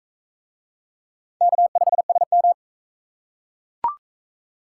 Morse code, heard as a keyed tone, about 41 wpm, 700 Hz. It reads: K5SM